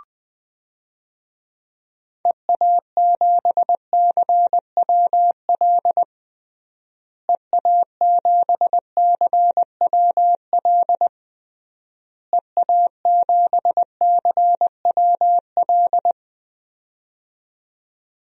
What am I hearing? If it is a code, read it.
Morse code, heard as a keyed tone, about 20 wpm, 700 Hz. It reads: EA7CWL EA7CWL EA7CWL
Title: DXCC-100 Call Signs - 6 Characters - Repeated 3x - 20wpm